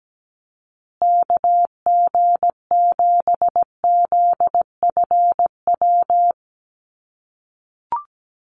Morse code, heard as a keyed tone, about 17 wpm, 700 Hz. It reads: KG7ZFW
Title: USA Call Signs Repeated 2x 17wpm